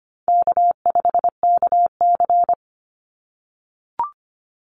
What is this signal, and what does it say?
Morse code, heard as a keyed tone, about 25 wpm, 700 Hz. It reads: K5KC
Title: Call Signs - 25wpm